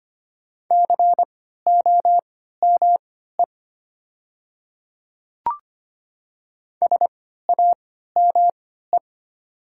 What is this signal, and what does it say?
Morse code, heard as a keyed tone, about 25 wpm, spaced as for 14 wpm, 700 Hz. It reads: COME SAME
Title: 3x Character Spacing - Top 200 Words - 25wpm